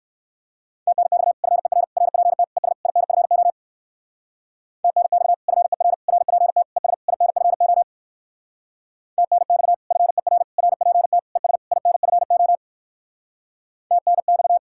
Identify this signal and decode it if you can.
Morse code, heard as a keyed tone, about 58 wpm, 700 Hz. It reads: TNX FER RPT ES INFO TNX FER RPT ES INFO TNX FER RPT ES INFO TNX